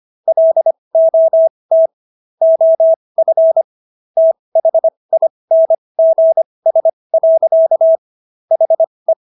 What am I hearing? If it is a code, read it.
Morse code, heard as a keyed tone, about 25 wpm, spaced as for 20 wpm, 650 Hz. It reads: LOT OF THINGS. HE